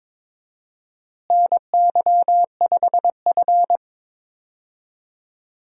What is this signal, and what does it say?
Morse code, heard as a keyed tone, about 22 wpm, 700 Hz. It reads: NY5F
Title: USA Call Signs - 4 Characters - Repeated 3x - 22wpm